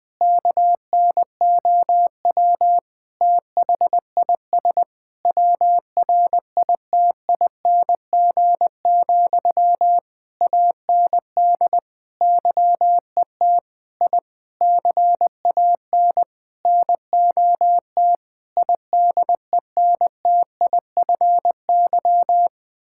Morse code, heard as a keyed tone, about 20 wpm, 700 Hz. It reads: KNOW THIS WRITING, AND YET I CAN NOT IDENTIFY
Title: Sentences from Top 1000 Words - Review Part 6 - 20wpm